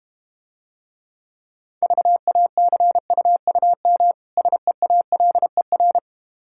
Morse code, heard as a keyed tone, about 32 wpm, 700 Hz. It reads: VACUUM SEALER